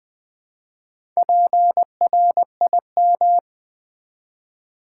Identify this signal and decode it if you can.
Morse code, heard as a keyed tone, about 20 wpm, 700 Hz. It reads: PRIM
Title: Build A Word - 1 Letter At A Time - 5-Letter Words - 20wpm